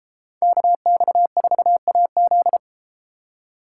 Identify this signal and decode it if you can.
Morse code, heard as a keyed tone, about 31 wpm, 700 Hz. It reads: KX4AZ